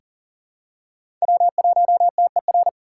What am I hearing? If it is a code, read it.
Morse code, heard as a keyed tone, about 39 wpm, 700 Hz. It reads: W1TER
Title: K1USN SST QSO Practice - 40wpm